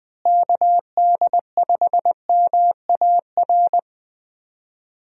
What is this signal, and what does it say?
Morse code, heard as a keyed tone, about 20 wpm, 700 Hz. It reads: KD5MAR
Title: USA Call Signs - 6 Characters - Repeated 3x - 20wpm